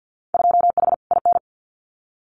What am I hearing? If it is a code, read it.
Morse code, heard as a keyed tone, about 50 wpm, 700 Hz. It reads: 2H IN
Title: Winter Field Day Exchanges Encore 50wpm